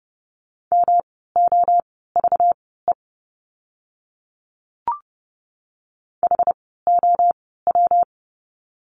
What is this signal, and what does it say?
Morse code, heard as a keyed tone, about 30 wpm, spaced as for 17 wpm, 700 Hz. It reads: MOVE HOW
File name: course / 30:10wpm / 237 Lesson 38f - Letter q - Words 30wpm